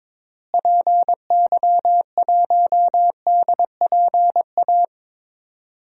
Morse code, heard as a keyed tone, about 22 wpm, 700 Hz. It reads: PY1DPA
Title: DXCC-100 Call Signs - 6 Characters - Repeated 2x - 22wpm